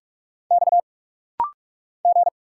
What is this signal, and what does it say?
Morse code, heard as a keyed tone, about 46 wpm, 700 Hz. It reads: X G
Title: Single Letters - Rapid-Fire 45wpm